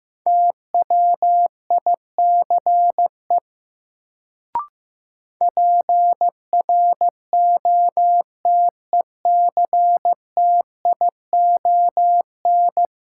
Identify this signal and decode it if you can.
Morse code, heard as a keyed tone, about 15 wpm, 700 Hz. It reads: TWICE PROTECTION
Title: Sentences from Top 1600 Words - New Word Drill - 15wpm